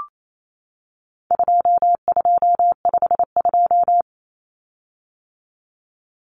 Morse code, heard as a keyed tone, about 28 wpm, 700 Hz. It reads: 2252
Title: Common Cut Numbers - 4-Digit 28wpm